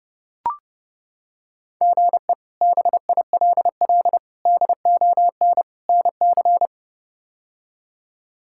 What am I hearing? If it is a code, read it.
Morse code, heard as a keyed tone, about 30 wpm, 700 Hz. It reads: GE BILL DON NC
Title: K1USN SST QSO Practice - 30wpm